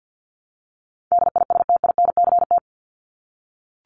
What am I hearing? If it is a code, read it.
Morse code, heard as a keyed tone, about 48 wpm, 700 Hz. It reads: DISTINCT
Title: Top 1501-1750 Words 50wpm